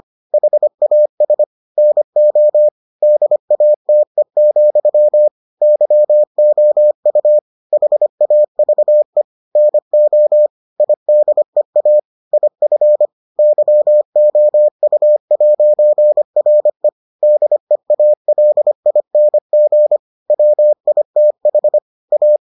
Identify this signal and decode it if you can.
Morse code, heard as a keyed tone, about 25 wpm, 600 Hz. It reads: HAS NO DATE, YOU HAVE NO IDEA IF YOU'RE DEALING WITH A